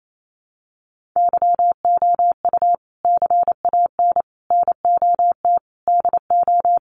Morse code, heard as a keyed tone, about 28 wpm, 700 Hz. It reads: YOU CAN NOT DO
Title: Sets of 5 Words Encore2 28wpm